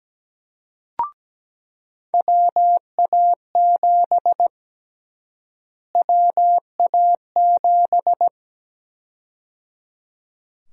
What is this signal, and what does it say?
Morse code, heard as a keyed tone, about 17 wpm, 700 Hz. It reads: WA7 WA7
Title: USA Call Sign Prefixes - Repeated 2x - 17wpm